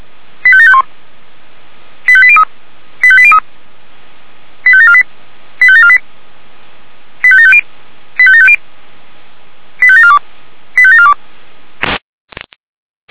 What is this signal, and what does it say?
Five-tone selective call (ZVEI), heard as quick runs of five tones, tones of about 70 ms. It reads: 85751 85803 85803 85748 85748 85750 85750 85742 85742